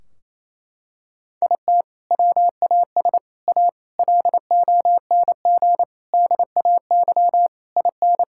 Morse code, heard as a keyed tone, about 28 wpm, 700 Hz. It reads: IT WAS A LONG DAY IN
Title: Sentences from Top 1600 Words 28wpm